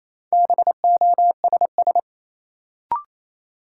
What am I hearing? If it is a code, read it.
Morse code, heard as a keyed tone, about 27 wpm, 700 Hz. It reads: BOSS
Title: Sentences from Top 700 Words - New Word Drill - 28wpm